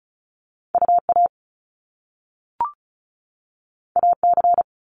Morse code, heard as a keyed tone, about 36 wpm, 700 Hz. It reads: UA AC
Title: Top Two Letter Combinations in Words 35wpm